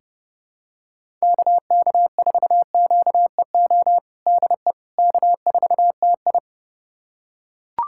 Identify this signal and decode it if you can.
Morse code, heard as a keyed tone, about 30 wpm, 700 Hz. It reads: KK4QEO DE K4TI
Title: USA Callsign DE Callsign 30wpm